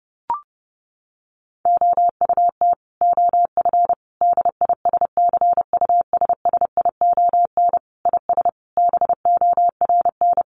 Morse code, heard as a keyed tone, about 30 wpm, 700 Hz. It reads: OUT OF DISCUSSION IS BORN
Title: Sentences from Top 1100 Words 30wpm